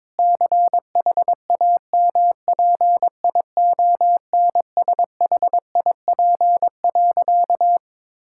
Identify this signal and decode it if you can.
Morse code, heard as a keyed tone, about 22 wpm, 700 Hz. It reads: CHAMPIONSHIP.
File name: practice / 22wpm / Sentences with 8-Word Phrases - 22wpm